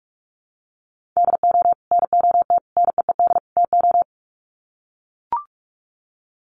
Morse code, heard as a keyed tone, about 44 wpm, 700 Hz. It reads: DO NOT NEED TO
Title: Sets of 4 Words 45wpm